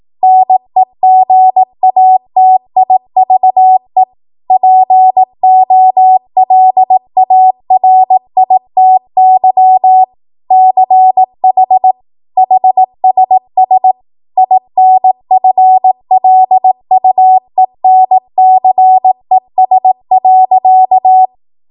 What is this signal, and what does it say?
Morse code, heard as a keyed tone, about 18 wpm, 750 Hz. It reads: NEGATIVE POLARITY CH HSS INFLUENCES.
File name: Morse / 18WPM